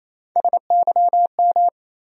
Morse code, heard as a keyed tone, about 28 wpm, 700 Hz. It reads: SYM
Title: USA Call Sign Postfixes - 3 Letters 28wpm